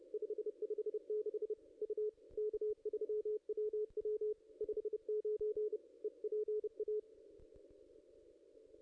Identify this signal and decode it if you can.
Morse code, heard as a keyed tone, about 32 wpm, 440 Hz. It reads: 556 U K3WW 59 EPA
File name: contests / ss-cw-k5tr-op-n5rz -0400